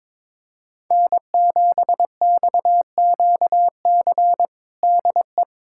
Morse code, heard as a keyed tone, about 22 wpm, 700 Hz. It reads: N7XQC DE
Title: USA Callsign DE Callsign - 5 Characters - 22wpm